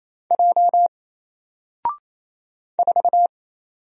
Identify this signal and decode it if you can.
Morse code, heard as a keyed tone, about 27 wpm, 700 Hz. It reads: J 4
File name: Single-Letter-Number-28wpm